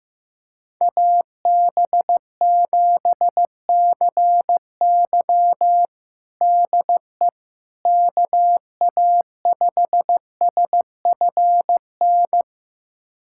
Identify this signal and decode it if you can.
Morse code, heard as a keyed tone, about 15 wpm, 700 Hz. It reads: AB7CY DE KA5SFN
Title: USA Callsign DE Callsign 15wpm